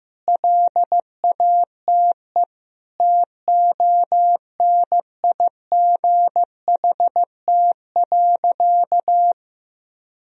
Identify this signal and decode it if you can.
Morse code, heard as a keyed tone, about 15 wpm, 700 Hz. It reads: LATE TONIGHT.
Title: Sentences from Top 1000 Words - Review Part 4 - 15wpm